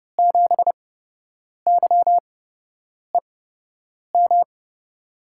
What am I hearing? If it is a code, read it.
Morse code, heard as a keyed tone, about 31 wpm, spaced as for 8 wpm, 700 Hz. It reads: 7YEM
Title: USA Call Signs - 6 Characters - 8x Character Spacing - 30wpm